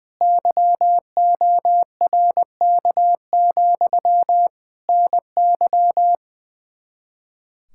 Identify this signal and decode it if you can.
Morse code, heard as a keyed tone, about 21 wpm, 700 Hz. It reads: YORK, NY
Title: K1USN SST QSO Practice 20wpm